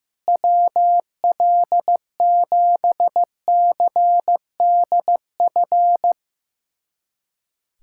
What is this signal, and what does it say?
Morse code, heard as a keyed tone, about 16 wpm, 700 Hz. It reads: WL7CDF
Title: USA Callsign DE Callsign - 6 Characters - 15wpm